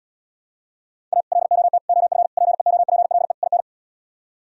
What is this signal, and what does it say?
Morse code, heard as a keyed tone, about 77 wpm, 700 Hz. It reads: A LOT OF PEOPLE IN